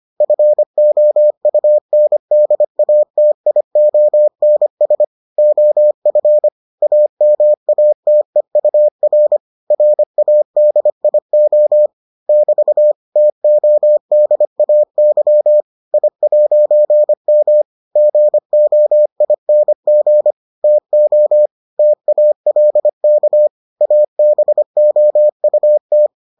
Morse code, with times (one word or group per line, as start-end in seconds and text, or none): 0.2-5.0 FOUNDATIONS
5.4-6.5 OF
6.8-9.4 AMATEUR
9.7-11.9 RADIO
12.3-12.9 <BT>
13.2-15.6 TODAY
15.9-17.6 I'M
18.0-20.3 GOING
20.6-21.5 TO
21.8-23.5 TALK
23.8-26.1 ABOUT